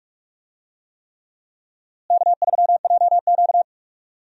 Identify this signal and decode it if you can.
Morse code, heard as a keyed tone, about 46 wpm, 700 Hz. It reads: K3JQ
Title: USA Call Signs - 4 Characters - Repeated 3x - 45wpm